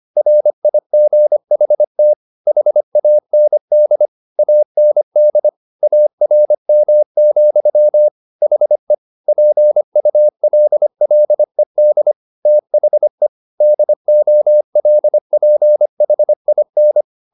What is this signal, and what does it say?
Morse code, heard as a keyed tone, about 25 wpm, 600 Hz. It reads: RIGHT HAND AND ARM, HE PULLED THE DOLPHIN